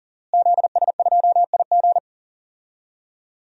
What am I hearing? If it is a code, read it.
Morse code, heard as a keyed tone, about 40 wpm, 700 Hz. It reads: ZS2IG
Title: DXCC-100 Call Signs - 5 Characters - Repeated 3x - 40wpm